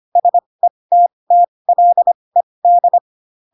Morse code, heard as a keyed tone, about 25 wpm, spaced as for 20 wpm, 700 Hz. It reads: SETTLED
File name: Pinocchio / Pinocchio ch2 15 25 ik1hge